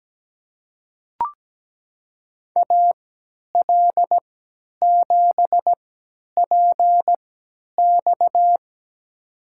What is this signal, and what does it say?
Morse code, heard as a keyed tone, about 17 wpm, spaced as for 10 wpm, 700 Hz. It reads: AL7PX
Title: USA Call Signs - 5 Characters - 3x Character Spacing - 17wpm